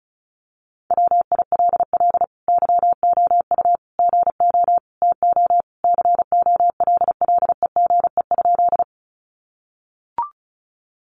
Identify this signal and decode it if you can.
Morse code, heard as a keyed tone, about 34 wpm, 700 Hz. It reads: WILL YOU GO TO COLLEGE?
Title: Sentences from Top 1300 Words 35wpm